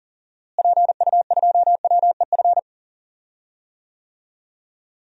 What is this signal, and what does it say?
Morse code, heard as a keyed tone, about 37 wpm, 700 Hz. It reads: PU2WEF
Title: DXCC-100 Call Signs - 6 Characters - Repeated 3x - 40wpm